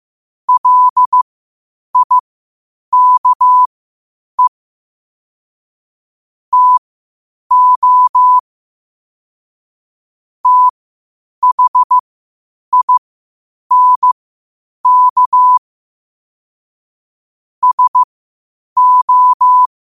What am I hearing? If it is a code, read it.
Morse code, heard as a keyed tone, about 15 wpm, spaced as for 8 wpm, 1000 Hz. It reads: LIKE TO THINK SO